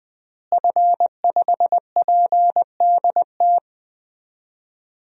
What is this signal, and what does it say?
Morse code, heard as a keyed tone, about 20 wpm, 700 Hz. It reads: F5PDT
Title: DXCC-100 Call Signs - 5 Characters - Repeated 2x - 20wpm